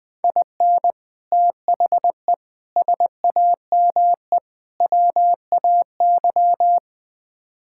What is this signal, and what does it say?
Morse code, heard as a keyed tone, about 20 wpm, 700 Hz. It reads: IN THE SAME WAY
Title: Build A Phrase - 4-Word Phrases - 20wpm